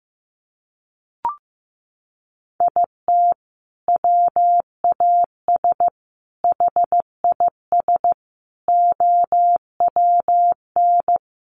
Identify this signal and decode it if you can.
Morse code, heard as a keyed tone, about 15 wpm, 700 Hz. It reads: IT WAS HIS OWN
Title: Sentences from Top 300 Words 15wpm